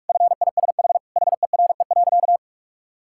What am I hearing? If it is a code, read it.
Morse code, heard as a keyed tone, about 46 wpm, 700 Hz. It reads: FISH HERE.
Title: Sentences from Top 1000 Words - Review Part 6 - 45wpm